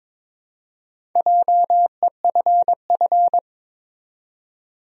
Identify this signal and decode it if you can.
Morse code, heard as a keyed tone, about 22 wpm, 700 Hz. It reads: JEFF
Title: USA Common Names - Max 4 Letters - 22wpm